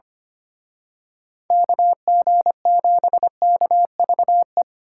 Morse code, heard as a keyed tone, about 29 wpm, 700 Hz. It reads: KG7KVE
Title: Speed Racing - USA 2x3 Call Signs 25wpm